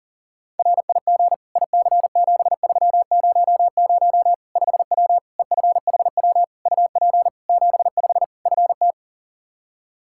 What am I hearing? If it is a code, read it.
Morse code, heard as a keyed tone, about 40 wpm, 700 Hz. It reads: RIG IC7300 5W EFHW UP 75 FT
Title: Practice QSO - Exchange 2 - Info - 40wpm